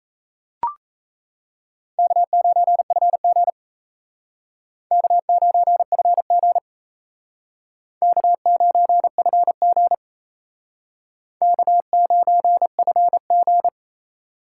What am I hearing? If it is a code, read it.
Morse code, heard as a keyed tone, about 35 wpm, 700 Hz. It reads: K9FG K9FG K9FG K9FG
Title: Speed Racing - USA 1x2 Call Signs 28wpm